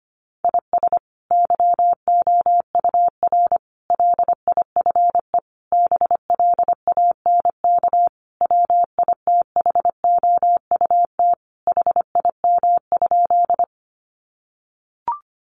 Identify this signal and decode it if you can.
Morse code, heard as a keyed tone, about 25 wpm, 700 Hz. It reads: IS YOUR LIFE BLANK WITHOUT HIM?
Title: Sentences from Top 1500 Words 25wpm